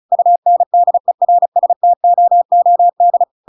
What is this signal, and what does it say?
Morse code, heard as a keyed tone, about 35 wpm, 700 Hz. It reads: UNDERSTOOD
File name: Pinocchio / Pinocchio ch8 35 35 ik1hge